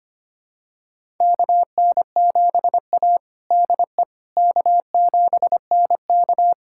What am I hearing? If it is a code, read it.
Morse code, heard as a keyed tone, about 25 wpm, 700 Hz. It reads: KN7A DE K7NK
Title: USA Callsign DE Callsign - 4 Characters - 25wpm